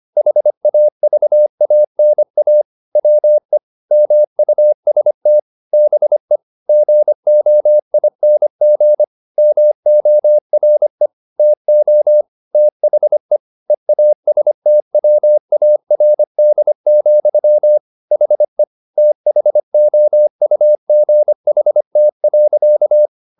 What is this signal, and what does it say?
Morse code, heard as a keyed tone, about 25 wpm, 600 Hz. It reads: HAVANA WE MUST BE GOING MORE TO THE EASTWARD, HE THOUGHT.
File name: the old man and the sea part 1 25wpm